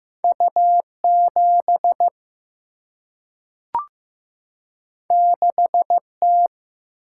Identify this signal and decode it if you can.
Morse code, heard as a keyed tone, about 15 wpm, 700 Hz. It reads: U 7 6 T
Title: Single-Letter Single-Number 15wpm